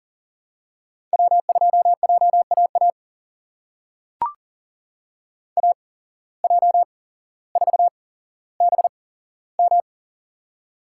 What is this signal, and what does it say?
Morse code, heard as a keyed tone, about 40 wpm, spaced as for 17 wpm, 700 Hz. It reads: W2JAA AJ4BM
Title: USA Call Signs - 5 Characters - 8x Character Spacing - 40wpm